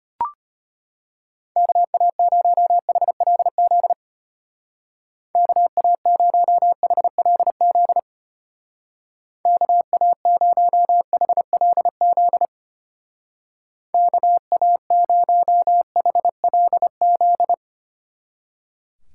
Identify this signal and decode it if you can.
Morse code, heard as a keyed tone, about 32 wpm, 700 Hz. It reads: KA0HLZ KA0HLZ KA0HLZ KA0HLZ